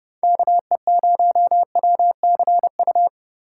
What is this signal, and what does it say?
Morse code, heard as a keyed tone, about 39 wpm, 700 Hz. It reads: KE0WCU